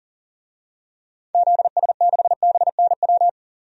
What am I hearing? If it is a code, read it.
Morse code, heard as a keyed tone, about 40 wpm, 700 Hz. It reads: ZS6BNW
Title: DXCC-100 Call Signs - 6 Characters - 40wpm